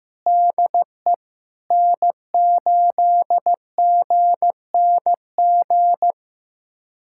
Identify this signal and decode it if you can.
Morse code, heard as a keyed tone, about 15 wpm, 700 Hz. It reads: DE N8GNG
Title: USA Callsign DE Callsign 15wpm